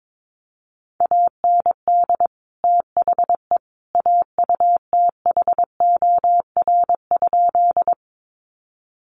Morse code, heard as a keyed tone, about 22 wpm, 700 Hz. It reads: AND THE AUTHOR?
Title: Sentences from Top 1200 Words 22wpm